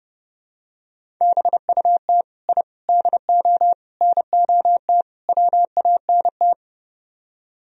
Morse code, heard as a keyed tone, about 32 wpm, 700 Hz. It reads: BUT I DO NOT WANT